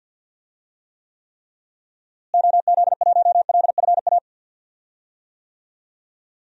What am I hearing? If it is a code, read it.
Morse code, heard as a keyed tone, about 46 wpm, 700 Hz. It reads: OZ1LFA
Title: DXCC-100 Call Signs - 6 Characters - Repeated 3x - 50wpm